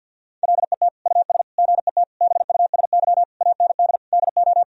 Morse code, heard as a keyed tone, about 50 wpm, 700 Hz. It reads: LET US GET BUSY AND DO
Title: Sentences from Top 1000 Words - Review Part 3 - 50wpm